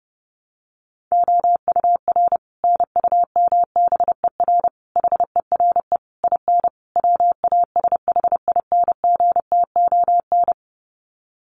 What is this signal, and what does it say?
Morse code, heard as a keyed tone, about 30 wpm, 700 Hz. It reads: OUR NUMBER HERE IN WASHINGTON
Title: Sets of 5 Words Encore2 30wpm